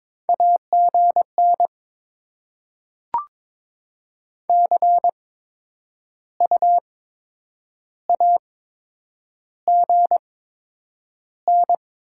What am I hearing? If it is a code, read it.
Morse code, heard as a keyed tone, about 22 wpm, spaced as for 8 wpm, 700 Hz. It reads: AGN CU AGN
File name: Top 20 QSO Elements - 8x Character Spacing - 22wpm